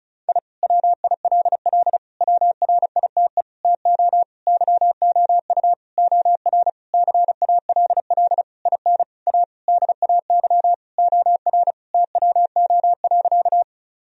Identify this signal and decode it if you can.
Morse code, heard as a keyed tone, about 36 wpm, 700 Hz. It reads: I WILL WRITE TO YOU OR CALL IN A DAY OR TWO.
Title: Sentences from Top 1000 Words - Review Part 1 - 35wpm